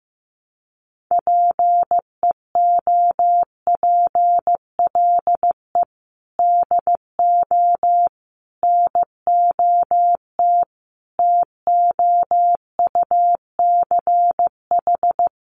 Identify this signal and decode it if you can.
Morse code, heard as a keyed tone, about 15 wpm, 700 Hz. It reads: PEOPLE DO NOT TOUCH
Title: Sentences from Top 1200 Words 15wpm